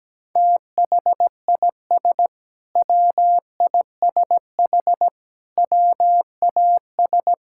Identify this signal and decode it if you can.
Morse code, heard as a keyed tone, about 17 wpm, 700 Hz. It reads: THIS WISH WAS
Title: Sentences from Top 1600 Words 17wpm